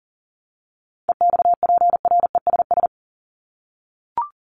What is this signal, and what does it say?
Morse code, heard as a keyed tone, about 40 wpm, 700 Hz. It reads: EXPRESS
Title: ED Suffix - Top 100 Words with Base Words 40wpm